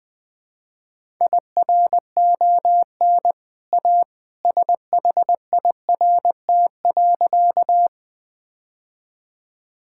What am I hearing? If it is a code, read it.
Morse code, heard as a keyed tone, about 20 wpm, 700 Hz. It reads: IRON A SHIRT.
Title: Easy Sentences with Hints - 20wpm